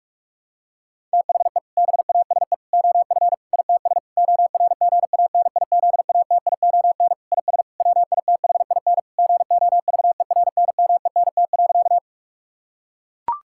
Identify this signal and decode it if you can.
Morse code, heard as a keyed tone, about 41 wpm, 700 Hz. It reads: THE BASE OF ITS ORGANIZATION IS WITHIN GOVERNMENT.
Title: Sentences from Top 1000 Words - Review Part 5 - 45wpm